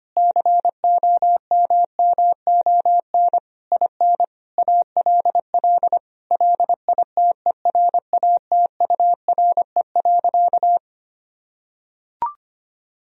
Morse code, heard as a keyed tone, about 25 wpm, 700 Hz. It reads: COMMON IN ALL LITERATURE.
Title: Sentences from Top 1000 Words - Review Part 3 - 25wpm